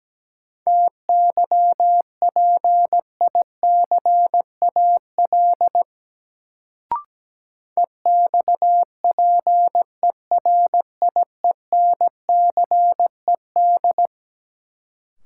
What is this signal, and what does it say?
Morse code, heard as a keyed tone, about 17 wpm, 700 Hz. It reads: TYPICAL EXPERIENCED
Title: Sentences from Top 1100 Words - New Word Drill Alt - 17wpm